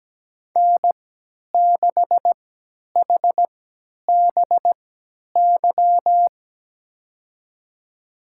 Morse code, heard as a keyed tone, about 17 wpm, spaced as for 10 wpm, 700 Hz. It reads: N6HBY
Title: USA Call Signs - 5 Characters - 3x Character Spacing - 17wpm